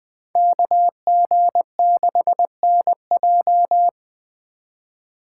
Morse code, heard as a keyed tone, about 20 wpm, 700 Hz. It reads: KG6NJ